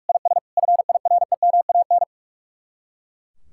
Morse code, heard as a keyed tone, about 47 wpm, 700 Hz. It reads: IS FIREMAN
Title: Speed-Racing Common QSO Phrases Encore - 45wpm